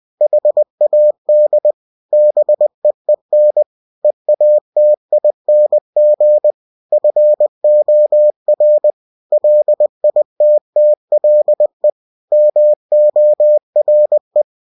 Morse code, heard as a keyed tone, about 20 wpm, 600 Hz. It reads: HAD BEEN EATING FOR LITTLE MORE
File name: cw-books / winnie-the-pooh / winnie-20wpm-page-0093